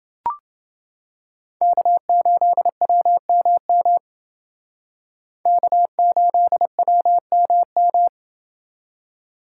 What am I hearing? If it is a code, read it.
Morse code, heard as a keyed tone, about 28 wpm, 700 Hz. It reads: K8WMM K8WMM